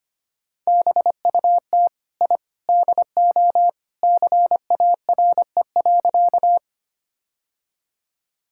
Morse code, heard as a keyed tone, about 25 wpm, 700 Hz. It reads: BUT I DO CARE.
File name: practice / 25wpm / Sentences with 3-Word Phrases - 25wpm